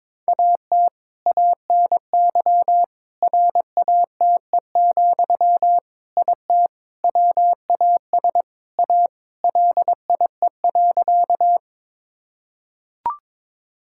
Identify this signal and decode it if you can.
Morse code, heard as a keyed tone, about 23 wpm, 700 Hz. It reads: AT ANY RATE, IT WAS A LIE.
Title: Sentences from Top 1600 Words 22wpm